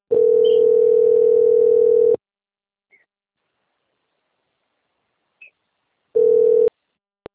Ringback tone (North America), 440 + 480 Hz.
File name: out- - - - - .31562